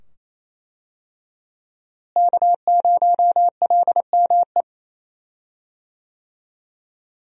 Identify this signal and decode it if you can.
Morse code, heard as a keyed tone, about 28 wpm, 700 Hz. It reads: K0LME
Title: USA Call Signs - 5 Characters - Repeated 2x - 28wpm